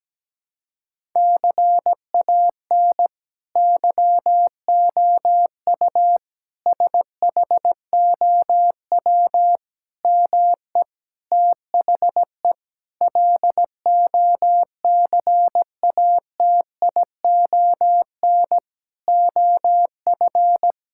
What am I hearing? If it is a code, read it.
Morse code, heard as a keyed tone, about 17 wpm, 700 Hz. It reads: CAN YOU SHOW ME THE LOCATION OF
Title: Sentences from Top 1000 Words - Review Part 6 - 17wpm